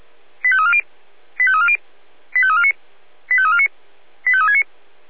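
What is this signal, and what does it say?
Five-tone selective call (ZVEI), heard as quick runs of five tones, tones of about 80 ms. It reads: 85340 85340 85349 85349 85369
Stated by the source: ZVEI five-tone call 85346 received on an SDR.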